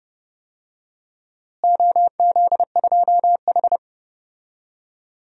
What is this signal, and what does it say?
Morse code, heard as a keyed tone, about 30 wpm, 700 Hz. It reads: OZ2H
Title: DXCC-100 Call Signs - 4 Characters - Repeated 3x - 30wpm